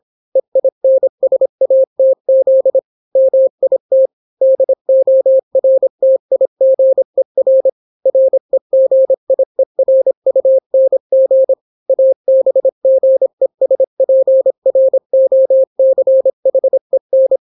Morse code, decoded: EINSATZ MIT DORTIGER REGIERUNG ABGESPROCHEN